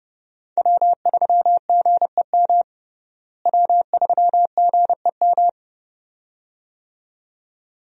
Morse code, heard as a keyed tone, about 32 wpm, 700 Hz. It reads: W3GEM W3GEM
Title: USA Call Signs - 5 Characters - Repeated 2x - 30wpm